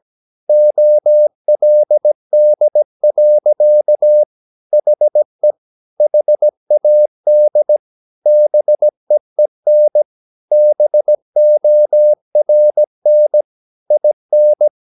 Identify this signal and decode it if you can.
Morse code, heard as a keyed tone, about 17 wpm, 600 Hz. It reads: OLD. HE HAD BEEN BORN IN
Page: Romanen Never af Ken Follet på engelsk.